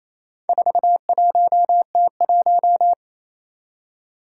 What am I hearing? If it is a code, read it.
Morse code, heard as a keyed tone, about 27 wpm, 700 Hz. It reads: 41T1
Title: Four Challenging Letter-Number Combinations 28wpm